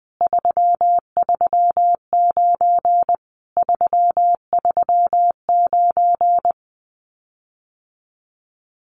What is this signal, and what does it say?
Morse code, heard as a keyed tone, about 20 wpm, 700 Hz. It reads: 339 339
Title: Common QSO Phrases Encore 20wpm